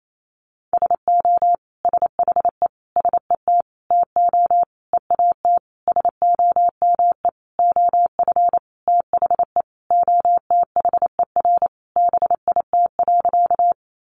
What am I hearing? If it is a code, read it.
Morse code, heard as a keyed tone, about 28 wpm, 690 Hz. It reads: SO SHE SET TO EAT SOME OF THE OTHER BIT.
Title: Sentences from Top 900 Words 28wpm